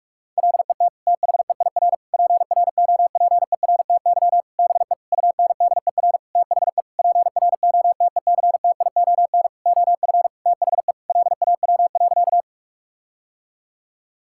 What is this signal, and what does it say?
Morse code, heard as a keyed tone, about 46 wpm, 700 Hz. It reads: LET THEIR PROPERTY BE UNDER THE PROTECTION OF THE LAW.
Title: Sentences from Top 1600 Words 45wpm